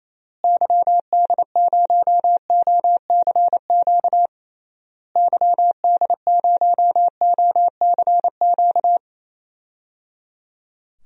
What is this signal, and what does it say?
Morse code, heard as a keyed tone, about 28 wpm, 700 Hz. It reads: YD0OCQ YD0OCQ
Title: DXCC-100 Call Signs - 6 Characters - Repeated 3x - 28wpm